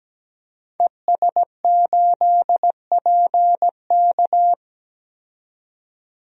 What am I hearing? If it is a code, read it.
Morse code, heard as a keyed tone, about 17 wpm, 700 Hz. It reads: ES8PK
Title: DXCC-100 Call Signs - 5 Characters - 17wpm